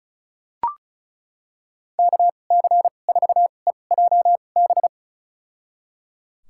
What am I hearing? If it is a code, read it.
Morse code, heard as a keyed tone, about 34 wpm, spaced as for 25 wpm, 700 Hz. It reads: KC4EJB
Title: USA Call Signs - 6 Characters - 2x Character Spacing - 35wpm